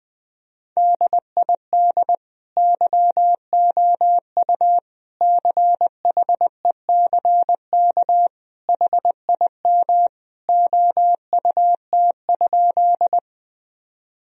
Morse code, heard as a keyed tone, about 21 wpm, 700 Hz. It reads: DID YOU CHECK HIM OUT?
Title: Sentences from Top 1000 Words - Review Part 3 - 20wpm